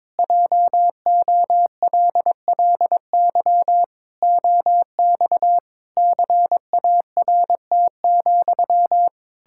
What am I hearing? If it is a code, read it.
Morse code, heard as a keyed tone, about 21 wpm, 700 Hz. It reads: JOLLY OX CART,
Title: Pangram Challenge v2 - 22wpm